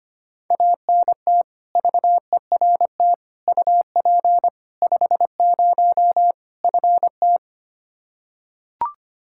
Morse code, heard as a keyed tone, about 25 wpm, 700 Hz. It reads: ANT VERT UP 50 FT
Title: Practice QSO - Exchange 2 - Info - 25wpm